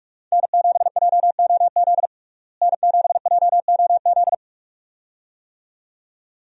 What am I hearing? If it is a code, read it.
Morse code, heard as a keyed tone, about 41 wpm, 700 Hz. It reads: N7JOZ N7JOZ
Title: USA Call Signs - 5 Characters - Repeated 2x - 45wpm